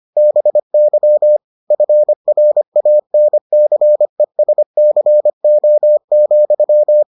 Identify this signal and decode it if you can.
Morse code, heard as a keyed tone, about 25 wpm, 600 Hz. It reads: BY FRANCESCO,